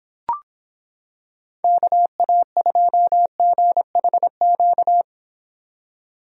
Morse code, heard as a keyed tone, about 26 wpm, 700 Hz. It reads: KA2GHQ